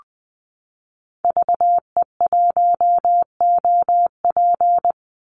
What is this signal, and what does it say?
Morse code, heard as a keyed tone, about 20 wpm, 700 Hz. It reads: VE1OP
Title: Call Signs Repeated 3x - 20wpm